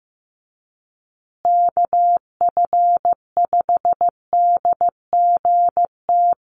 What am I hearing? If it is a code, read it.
Morse code, heard as a keyed tone, about 15 wpm, 700 Hz. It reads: KF5DGT
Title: USA Call Signs Repeated 3x 15wpm